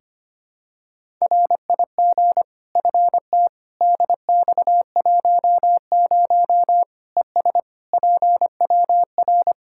Morse code, heard as a keyed tone, about 25 wpm, 700 Hz. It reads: RIG FT DX10 ES PWR
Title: Practice QSO - Exchange 2 - Info - 25wpm